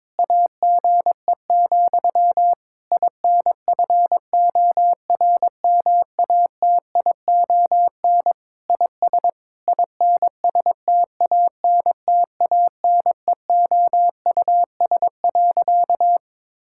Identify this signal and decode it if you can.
Morse code, heard as a keyed tone, about 22 wpm, 700 Hz. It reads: AGE, INFORMATION IS INSTANTANEOUS.
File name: Difficult Sentences with Hints - 22wpm